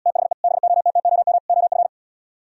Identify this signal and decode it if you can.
Morse code, heard as a keyed tone, about 77 wpm, 700 Hz. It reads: THE BOTTOM OF